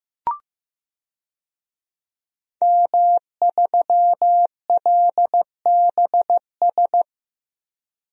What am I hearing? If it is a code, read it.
Morse code, heard as a keyed tone, about 15 wpm, 700 Hz. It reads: M3LBS
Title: DXCC-100 Call Signs - 5 Characters - 15wpm